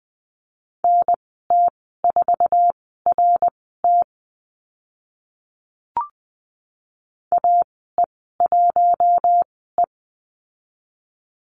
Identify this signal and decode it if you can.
Morse code, heard as a keyed tone, about 20 wpm, spaced as for 14 wpm, 700 Hz. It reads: NT4RT AE1E
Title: 062 Lesson 11g - Letter h - Call Signs 20wpm